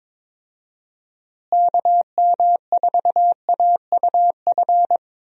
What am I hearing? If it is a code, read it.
Morse code, heard as a keyed tone, about 22 wpm, 700 Hz. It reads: KM4AUF
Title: USA Call Signs - 6 Characters - Repeated 3x - 22wpm